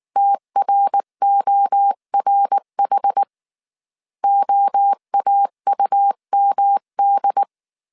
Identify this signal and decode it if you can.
Morse code, heard as a keyed tone, about 19 wpm, 800 Hz. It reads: TRORH OAUMD